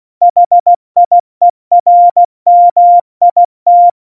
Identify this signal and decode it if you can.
Morse code, heard as a keyed tone, about 16 wpm, 700 Hz. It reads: HIERMIT
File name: Morsekurs / Witze / Witz 24. September 2019, 70 F80